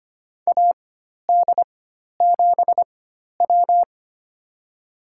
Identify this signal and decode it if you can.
Morse code, heard as a keyed tone, about 25 wpm, spaced as for 12 wpm, 700 Hz. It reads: AD7W
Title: USA Call Signs - 4 Characters - 4x Character Spacing - 25wpm